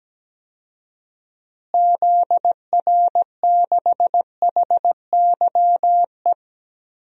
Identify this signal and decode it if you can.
Morse code, heard as a keyed tone, about 17 wpm, 700 Hz. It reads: ZR6HYE
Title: DXCC-100 Call Signs - 6 Characters - Repeated 2x - 17wpm